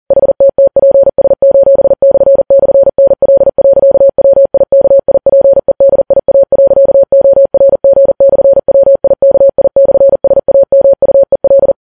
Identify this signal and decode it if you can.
Morse code, heard as a keyed tone, about 40 wpm, 550 Hz. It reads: HTTPS://NL.WIKIPEDIA.ORG/WIKI/SAMUEL
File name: Files Web tls / Morse on9fv / morse40kps